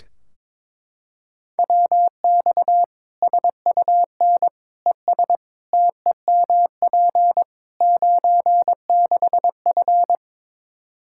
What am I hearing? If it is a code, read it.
Morse code, heard as a keyed tone, about 22 wpm, 700 Hz. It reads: WX SUN ES TEMP 96F
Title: Practice QSO - Exchange 2 - Info - 22wpm